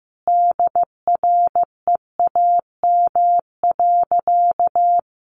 Morse code, heard as a keyed tone, about 15 wpm, 700 Hz. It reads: DREAM.
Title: Sentences from Top 1500 Words 15wpm